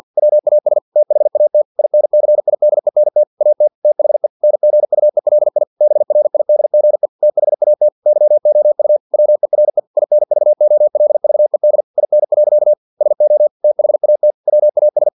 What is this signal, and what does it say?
Morse code, heard as a keyed tone, about 50 wpm, 600 Hz. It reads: WAS THAT INCIDENT AT THE NGUELI BRIDGE THAT YOU WERE INVOLVED IN. SO THAT WAS
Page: Never 49wpm